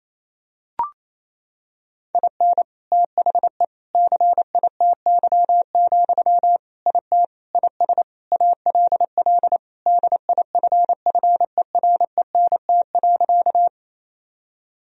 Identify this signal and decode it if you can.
Morse code, heard as a keyed tone, about 28 wpm, 700 Hz. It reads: IN THE CITY, IT IS ALL DIFFERENT.
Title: Sentences from Top 1000 Words - Review Part 3 - 28wpm